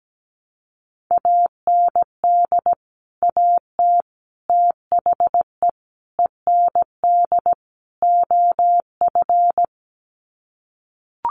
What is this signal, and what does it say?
Morse code, heard as a keyed tone, about 17 wpm, 700 Hz. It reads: AND AT THE END OF
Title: Sets of 5 Words 17wpm